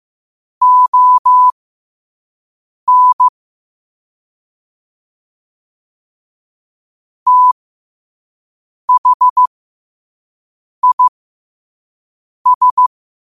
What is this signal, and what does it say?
Morse code, heard as a keyed tone, about 15 wpm, spaced as for 5 wpm, 1000 Hz. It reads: ON THIS
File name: qotdfiles / 5wpm